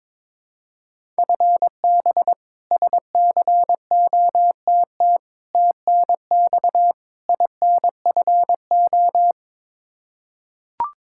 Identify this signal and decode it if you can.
Morse code, heard as a keyed tone, about 22 wpm, 700 Hz. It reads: FB SCOTT TNX INFO
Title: K1USN SST QSO Practice 22wpm